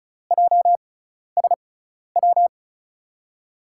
Morse code, heard as a keyed tone, about 36 wpm, spaced as for 12 wpm, 700 Hz. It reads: JSW
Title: USA Call Signs - 5 Characters - 6x Character Spacing - 35wpm